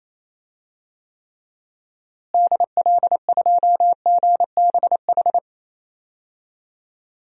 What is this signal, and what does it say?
Morse code, heard as a keyed tone, about 28 wpm, 700 Hz. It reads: DL2GBH
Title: DXCC-100 Call Signs - 6 Characters - Repeated 3x - 28wpm